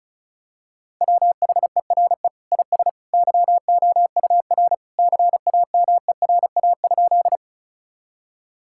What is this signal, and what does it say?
Morse code, heard as a keyed tone, about 34 wpm, 700 Hz. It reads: WHERE IS YOUR CAMERA?
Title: Sentences from Top 1000 Words - Review Part 5 - 35wpm